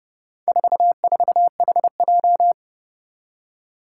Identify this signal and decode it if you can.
Morse code, heard as a keyed tone, about 30 wpm, 700 Hz. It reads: 44HJ
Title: Four Challenging Letter-Number Combinations 30wpm